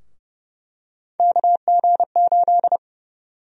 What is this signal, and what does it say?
Morse code, heard as a keyed tone, about 31 wpm, 700 Hz. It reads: KG8
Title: USA Call Sign Prefixes - 30wpm